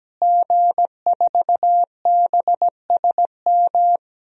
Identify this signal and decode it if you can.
Morse code, heard as a keyed tone, about 17 wpm, 700 Hz. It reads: G4BSM
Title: DXCC-100 Call Signs - 5 Characters - Repeated 3x - 17wpm